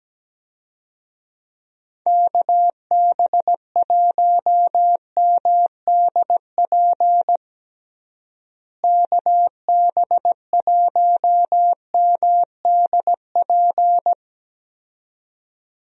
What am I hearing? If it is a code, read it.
Morse code, heard as a keyed tone, about 17 wpm, 700 Hz. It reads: KB1MDP KB1MDP
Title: USA Call Signs - 6 Characters - Repeated 2x - 17wpm